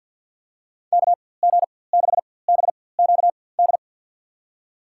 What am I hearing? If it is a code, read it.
Morse code, heard as a keyed tone, about 48 wpm, spaced as for 23 wpm, 700 Hz. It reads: KG6BQD
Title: USA Call Signs - 4x Character Spacing - 50wpm